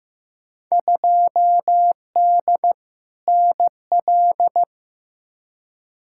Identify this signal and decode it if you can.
Morse code, heard as a keyed tone, about 15 wpm, 700 Hz. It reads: 2D NL
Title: Field Day Exchanges Encore - 15wpm